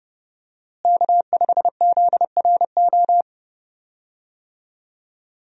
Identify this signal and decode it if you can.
Morse code, heard as a keyed tone, about 30 wpm, 700 Hz. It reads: K5ZRO
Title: USA Call Signs - 5 Characters - Repeated 3x - 30wpm